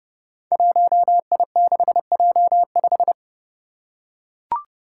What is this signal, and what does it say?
Morse code, heard as a keyed tone, about 31 wpm, 700 Hz. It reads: 1I6J5
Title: Five Challenging Letter-Number Combinations 30wpm